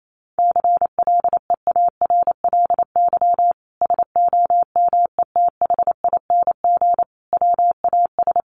Morse code, heard as a keyed tone, about 28 wpm, 700 Hz. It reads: CLEARLY SOMETHING WAS
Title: Sentences from Top 1300 Words 28wpm